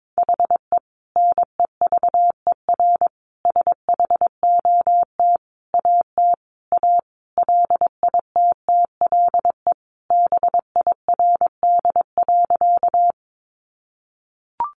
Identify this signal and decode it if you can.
Morse code, heard as a keyed tone, about 22 wpm, 700 Hz. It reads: HE NEVER SHOT AT A LITTLE BIRD.
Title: Sentences from Top 1100 Words 22wpm